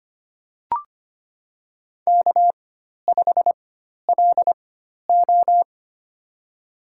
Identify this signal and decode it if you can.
Morse code, heard as a keyed tone, about 25 wpm, spaced as for 12 wpm, 700 Hz. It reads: K5LO
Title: USA Call Signs - 4 Characters - 4x Character Spacing - 25wpm